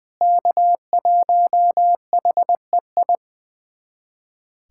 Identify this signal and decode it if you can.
Morse code, heard as a keyed tone, about 19 wpm, 700 Hz. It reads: K1HEI
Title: USA Callsign DE Callsign - 5 Characters - 20wpm